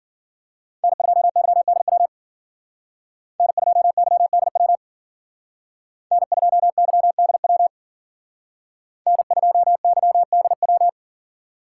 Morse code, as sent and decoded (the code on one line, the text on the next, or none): -. ..--- -.-- -.. .-- / -. ..--- -.-- -.. .-- / -. ..--- -.-- -.. .-- / -. ..--- -.-- -.. .--
N2YDW N2YDW N2YDW N2YDW